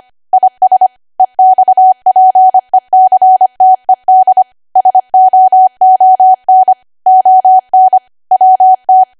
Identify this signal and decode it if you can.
Morse code, heard as a keyed tone, about 25 wpm, 750 Hz. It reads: IS EXPECTED SOON ON WT
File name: cwa / W1AW-Level3 / 25WPM